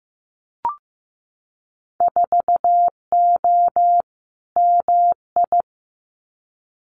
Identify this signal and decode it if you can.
Morse code, heard as a keyed tone, about 15 wpm, 700 Hz. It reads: 4O MI
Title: Winter Field Day Exchanges Encore 15wpm